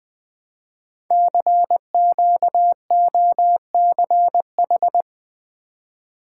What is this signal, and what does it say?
Morse code, heard as a keyed tone, about 20 wpm, 700 Hz. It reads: CQOCH
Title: Code Groups - 5 Letter:Numbers - 20wpm